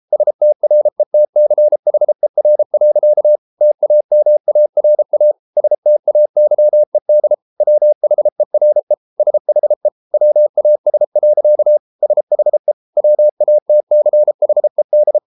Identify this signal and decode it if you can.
Morse code, heard as a keyed tone, about 33 wpm, 600 Hz. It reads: STRETCHER. TAMARA STAYED WHERE SHE WAS. SHE WATCHED